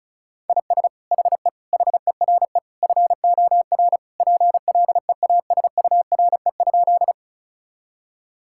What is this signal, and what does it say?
Morse code, heard as a keyed tone, about 36 wpm, 700 Hz. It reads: IS HE HERE FOR PLEASURE?
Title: Sentences from Top 1600 Words 35wpm